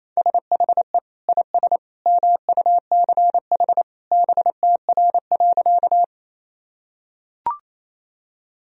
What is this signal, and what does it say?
Morse code, heard as a keyed tone, about 28 wpm, 700 Hz. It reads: SHE IS MUCH BTR.
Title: Sentences from Top 400 Words and Abbreviations 28wpm